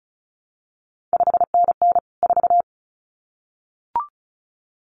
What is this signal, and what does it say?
Morse code, heard as a keyed tone, about 36 wpm, 700 Hz. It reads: 5NN 4
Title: CQ WW DX Contest Exchange Encore - 35wpm